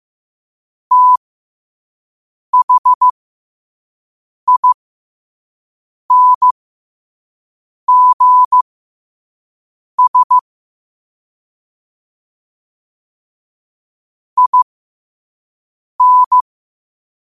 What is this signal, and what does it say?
Morse code, heard as a keyed tone, about 15 wpm, spaced as for 5 wpm, 1000 Hz. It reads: THINGS IN
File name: qotdfiles / 5wpm